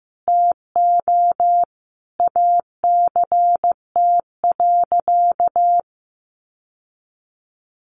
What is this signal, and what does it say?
Morse code, heard as a keyed tone, about 15 wpm, 700 Hz. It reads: TO ACT.
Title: Sentences from Top 1200 Words 15wpm